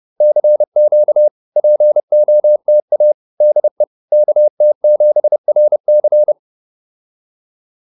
Morse code, heard as a keyed tone, about 30 wpm, 600 Hz. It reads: CQ POTA DE KT7RC